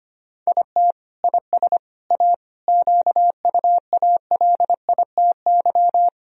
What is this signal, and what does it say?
Morse code, heard as a keyed tone, about 25 wpm, 700 Hz. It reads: IT IS A QUALITY
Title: Sentences from Top 1000 Words - Review Part 4 - 25wpm